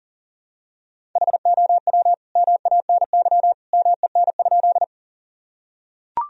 Morse code, heard as a keyed tone, about 40 wpm, 700 Hz. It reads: HOW MANY MEN?
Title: Sentences from Top 1000 Words - Review Part 1 - 40wpm